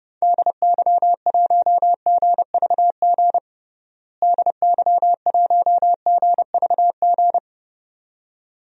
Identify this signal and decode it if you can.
Morse code, heard as a keyed tone, about 30 wpm, 700 Hz. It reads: DY1GVG DY1GVG
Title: DXCC-100 Call Signs - 6 Characters - Repeated 2x - 30wpm